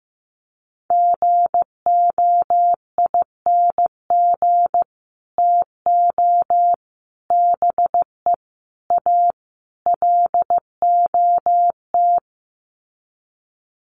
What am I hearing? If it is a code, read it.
Morse code, heard as a keyed tone, about 15 wpm, 700 Hz. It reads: GOING TO BE A LOT
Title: Sets of 5 Words Encore 15wpm